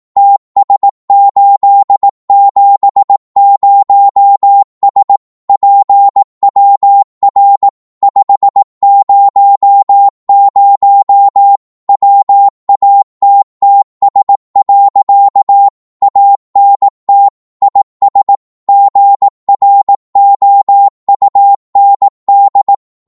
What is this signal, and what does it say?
Morse code, heard as a keyed tone, about 18 wpm, 800 Hz. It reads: TS870S PWR 500 WATTS. ANT IS GROUND
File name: Morse / qso-18wpm